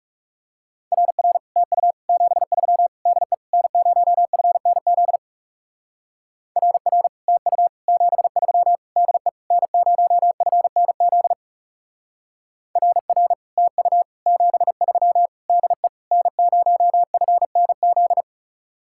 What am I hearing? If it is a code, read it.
Morse code, heard as a keyed tone, about 38 wpm, 700 Hz. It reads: RR TU 73 DE N0FNZ RR TU 73 DE N0FNZ RR TU 73 DE N0FNZ